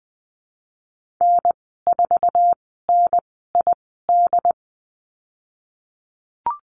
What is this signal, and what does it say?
Morse code, heard as a keyed tone, about 20 wpm, spaced as for 15 wpm, 700 Hz. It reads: N4NID
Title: 069 Lesson 12g - Letter d - Call Signs 20wpm